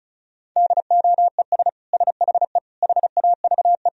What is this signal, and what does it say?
Morse code, heard as a keyed tone, about 36 wpm, 700 Hz. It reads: DOES SHE HAVE
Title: Sentences from Top 1000 Words - Review Part 2 - 35wpm